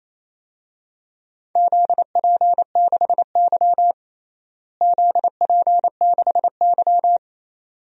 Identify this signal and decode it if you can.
Morse code, heard as a keyed tone, about 28 wpm, 700 Hz. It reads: ZP6Y ZP6Y
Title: DXCC-100 Call Signs - 4 Characters - Repeated 2x - 28wpm